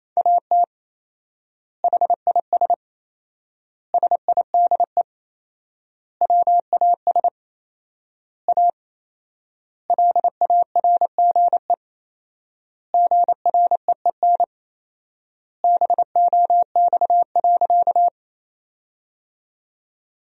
Morse code, heard as a keyed tone, about 28 wpm, 700 Hz. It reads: AT HIS SIDE WAS A LARGE GREEN BOX.
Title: 4x Word-Spacing - Sentences from Top 500 Words - 28wpm